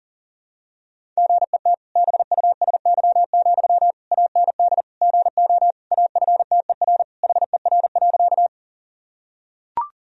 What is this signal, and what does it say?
Morse code, heard as a keyed tone, about 40 wpm, 700 Hz. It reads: GET BUSY, AND GO AFTER HER.
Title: Sentences from Top 1000 Words - Review Part 3 - 40wpm